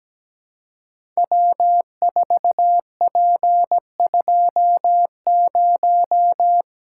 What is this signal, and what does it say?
Morse code, heard as a keyed tone, about 17 wpm, 700 Hz. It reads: W4P20
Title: Code Groups - 5 Letter:Numbers - 17wpm